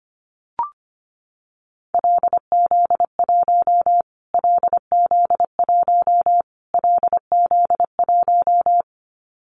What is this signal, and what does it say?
Morse code, heard as a keyed tone, about 25 wpm, 700 Hz. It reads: LZ1 LZ1 LZ1
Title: Callsign Prefixes for Top 25 DXCC - Repeated 3x - 25wpm